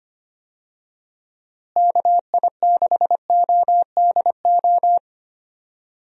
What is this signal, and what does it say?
Morse code, heard as a keyed tone, about 25 wpm, 700 Hz. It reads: KI6ODO
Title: USA Call Signs - 6 Characters - Repeated 3x - 25wpm